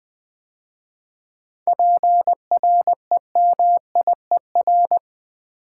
Morse code, heard as a keyed tone, about 20 wpm, 700 Hz. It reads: PREMIER
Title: 7 Letter Words - First 3 Letters Given - 20wpm